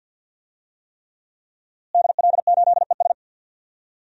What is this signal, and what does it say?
Morse code, heard as a keyed tone, about 52 wpm, 700 Hz. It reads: DL8ES
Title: DXCC-100 Call Signs - 5 Characters - Repeated 3x - 50wpm